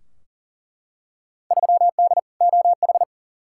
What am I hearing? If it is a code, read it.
Morse code, heard as a keyed tone, about 40 wpm, 700 Hz. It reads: 3D OH
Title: Field Day Exchanges - 40wpm